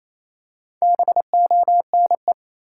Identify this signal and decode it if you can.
Morse code, heard as a keyed tone, about 28 wpm, 700 Hz. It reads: BONE
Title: Four Letter Words - 28wpm